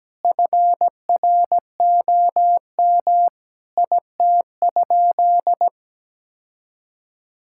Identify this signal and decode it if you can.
Morse code, heard as a keyed tone, about 17 wpm, 700 Hz. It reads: FROM IT?
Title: Sentences from Top 1000 Words - Review Part 5 - 17wpm